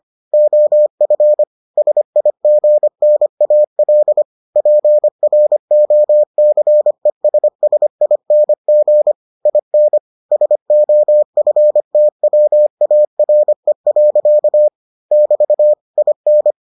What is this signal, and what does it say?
Morse code, heard as a keyed tone, about 25 wpm, 600 Hz. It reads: OF SIGNAL PROCESSING IN SOFTWARE. <BT> IN